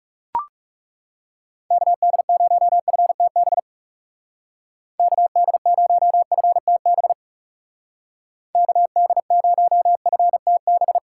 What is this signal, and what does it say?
Morse code, heard as a keyed tone, about 37 wpm, 700 Hz. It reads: KD0FTB KD0FTB KD0FTB
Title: Speed Racing - USA 2x3 Call Signs 30wpm